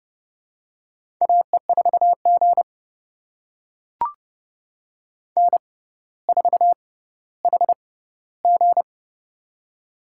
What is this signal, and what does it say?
Morse code, heard as a keyed tone, about 30 wpm, spaced as for 15 wpm, 700 Hz. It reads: AE4G N4HG
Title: USA Call Signs - 4 Characters - 6x Character Spacing - 30wpm